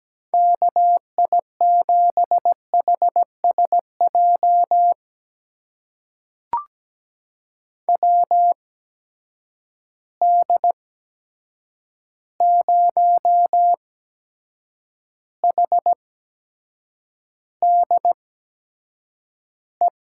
Morse code, heard as a keyed tone, about 17 wpm, spaced as for 7 wpm, 700 Hz. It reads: KI7HSJ WD0HDE
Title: USA Call Signs - 6 Characters - 8x Character Spacing - 17wpm